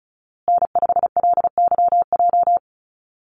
Morse code, heard as a keyed tone, about 34 wpm, 700 Hz. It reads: N5LYJ
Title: Call Signs Repeated 2x - 35wpm